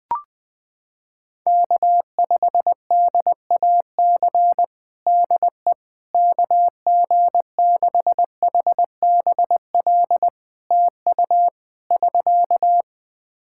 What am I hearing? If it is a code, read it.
Morse code, heard as a keyed tone, about 20 wpm, 700 Hz. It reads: K5DAC DE KG6HBL TU <SK>
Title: K1USN SST QSO Practice 20wpm